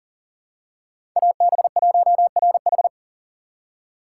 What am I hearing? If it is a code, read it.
Morse code, heard as a keyed tone, about 43 wpm, 690 Hz. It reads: AB1RH
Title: USA Call Signs - 5 Characters - 6x Character Spacing - 40wpm